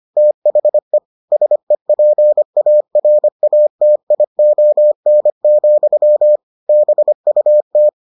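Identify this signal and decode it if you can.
Morse code, heard as a keyed tone, about 25 wpm, 600 Hz. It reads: THE SEPARATION, BUT
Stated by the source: Morse edition of the Foundations of Amateur Radio episodes.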